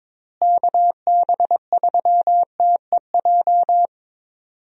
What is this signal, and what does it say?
Morse code, heard as a keyed tone, about 22 wpm, 700 Hz. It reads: KB3TEJ
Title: USA Call Signs - 6 Characters - Repeated 2x - 22wpm